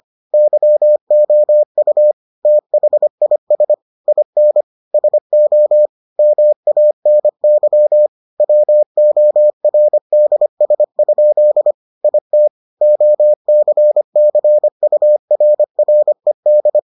Morse code, heard as a keyed tone, about 25 wpm, 600 Hz. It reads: YOU THIS IN SO MANY WORDS? IT OCCURRED